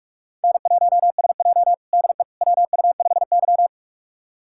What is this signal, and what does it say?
Morse code, heard as a keyed tone, about 47 wpm, 700 Hz. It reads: N1SJ DE WU5Y
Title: USA Callsign DE Callsign - 4 Characters - 45wpm